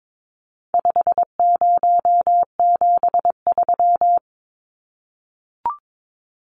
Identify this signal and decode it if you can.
Morse code, heard as a keyed tone, about 22 wpm, 700 Hz. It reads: 5073
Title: Numbers - 4-Digit 22wpm